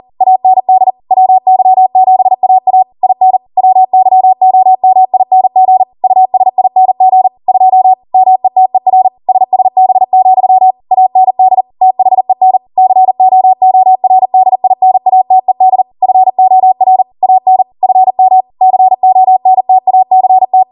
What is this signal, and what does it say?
Morse code, heard as a keyed tone, about 40 wpm, 750 Hz. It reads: AND WY7AA IN WYOMING USING 2 METER SSB, AND THEN COORDINATED FOR AN FM CONTACT